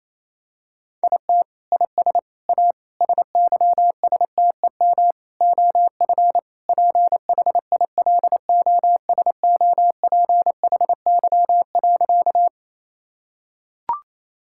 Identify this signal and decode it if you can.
Morse code, heard as a keyed tone, about 28 wpm, 700 Hz. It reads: IT IS A SYSTEM OF PHILOSOPHY.
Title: Sentences from Top 1000 Words - Review Part 6 - 28wpm